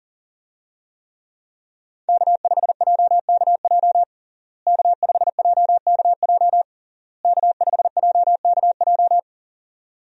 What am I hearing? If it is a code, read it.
Morse code, heard as a keyed tone, about 40 wpm, 700 Hz. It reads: K5JKJ K5JKJ K5JKJ
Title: USA Call Signs - 5 Characters - Repeated 3x - 40wpm